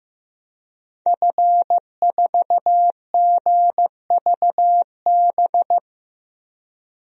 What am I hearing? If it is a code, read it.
Morse code, heard as a keyed tone, about 15 wpm, 700 Hz. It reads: F4GVB
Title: DXCC-100 Call Signs - 5 Characters - Repeated 3x - 15wpm